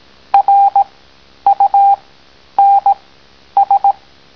Morse code, heard as a keyed tone, about 18 wpm, spaced as for 10 wpm, 800 Hz. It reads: RUNS